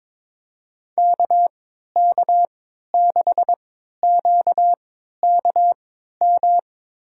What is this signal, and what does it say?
Morse code, heard as a keyed tone, about 22 wpm, spaced as for 13 wpm, 700 Hz. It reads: KK6QKM
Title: USA Call Signs - 3x Character Spacing - 22wpm